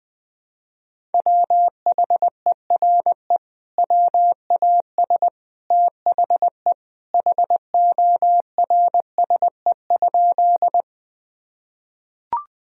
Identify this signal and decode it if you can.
Morse code, heard as a keyed tone, about 21 wpm, 700 Hz. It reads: WHERE WAS THE HORSE?
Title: Sentences from Top 1000 Words - Review Part 1 - 20wpm